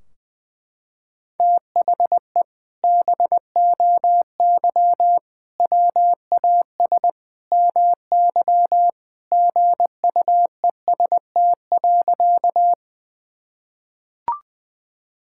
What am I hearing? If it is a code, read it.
Morse code, heard as a keyed tone, about 19 wpm, 700 Hz. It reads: THE BOY WAS MY GUEST.
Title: Sentences from Top 1600 Words 20wpm